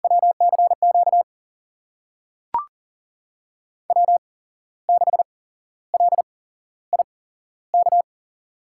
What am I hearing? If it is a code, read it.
Morse code, heard as a keyed tone, about 40 wpm, spaced as for 14 wpm, 700 Hz. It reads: WCQ W6LIK
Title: USA Call Signs - 5 Characters - 8x Character Spacing - 40wpm